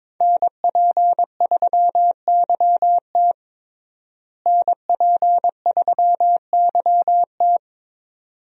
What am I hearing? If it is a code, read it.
Morse code, heard as a keyed tone, about 22 wpm, 700 Hz. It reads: NP3YT NP3YT
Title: USA Call Signs - 5 Characters - Repeated 2x - 22wpm